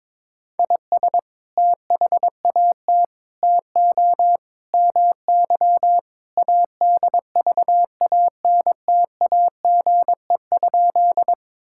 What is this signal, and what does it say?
Morse code, heard as a keyed tone, about 22 wpm, 700 Hz. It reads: IS THAT TO MY ADVANTAGE?
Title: Sentences from Top 1000 Words - Review Part 6 - 22wpm